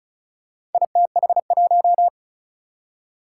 Morse code, heard as a keyed tone, about 36 wpm, 700 Hz. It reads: ITH1
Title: Four Challenging Letter-Number Combinations 35wpm